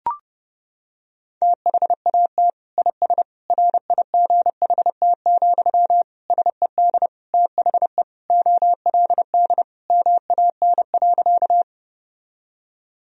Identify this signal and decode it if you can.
Morse code, heard as a keyed tone, about 30 wpm, 700 Hz. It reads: THAT IS RIGHT, SED THE OLD MAN.
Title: Sentences from Top 200 Words and Abbreviations 30wpm